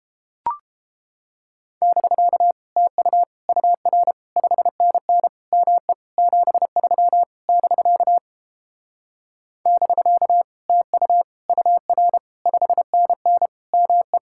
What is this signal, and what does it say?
Morse code, heard as a keyed tone, about 31 wpm, 700 Hz. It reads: <BK> TU UR 5NN ME 73 <BK> <BK> TU UR 5NN ME